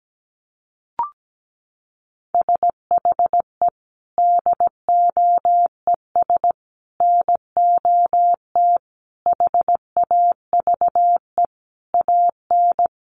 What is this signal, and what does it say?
Morse code, heard as a keyed tone, about 17 wpm, 700 Hz. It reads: SHE DOES NOT HAVE AN
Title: Sentences from Top 1200 Words 17wpm